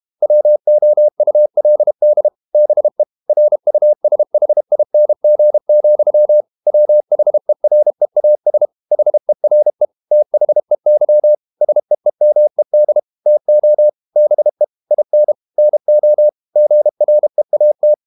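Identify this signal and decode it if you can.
Morse code, heard as a keyed tone, about 32 wpm, 600 Hz. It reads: WOULD BE RUSHING, WHEREAS HERE THEY SEEMED TO BE IN NO GREAT
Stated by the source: Morse code practice at 32 wpm from CW med Gnister, reading Ken Follett's novel Never in English.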